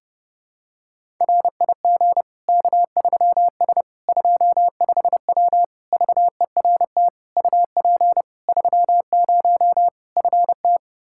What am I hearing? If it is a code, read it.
Morse code, heard as a keyed tone, about 30 wpm, 700 Hz. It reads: RIG K3S 25W VERT UP 30 FT
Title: Practice QSO - Exchange 2 - Info - 30wpm